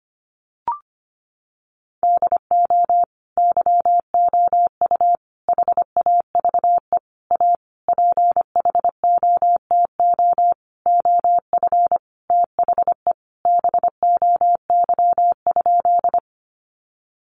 Morse code, heard as a keyed tone, about 25 wpm, 700 Hz. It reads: DO YOU HAVE A PHOTO OF THE BOY?
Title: Sentences from Top 1200 Words 25wpm